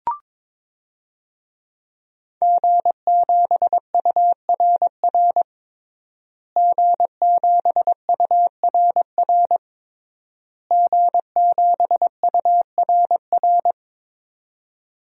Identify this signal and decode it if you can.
Morse code, heard as a keyed tone, about 22 wpm, 700 Hz. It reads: G7URR G7URR G7URR
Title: DXCC-100 Call Signs - 5 Characters - Repeated 3x - 22wpm